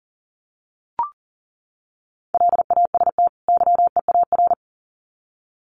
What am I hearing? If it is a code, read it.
Morse code, heard as a keyed tone, about 40 wpm, 700 Hz. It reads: LAST YEAR
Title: Sets of 2 Words 40wpm